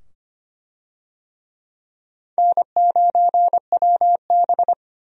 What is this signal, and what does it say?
Morse code, heard as a keyed tone, about 25 wpm, 700 Hz. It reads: N9WB
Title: USA Call Signs - 4 Characters - 25wpm